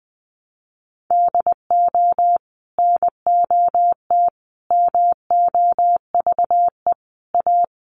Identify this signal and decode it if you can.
Morse code, heard as a keyed tone, about 20 wpm, 700 Hz. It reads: DO NOT MOVE A
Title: Sentences from Top 1300 Words 20wpm